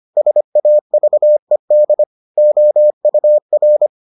Morse code, heard as a keyed tone, about 25 wpm, 600 Hz. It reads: SAVED OUR